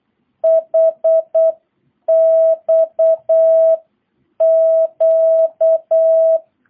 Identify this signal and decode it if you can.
Morse code, heard as a keyed tone, about 8 wpm, 660 Hz. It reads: HXQ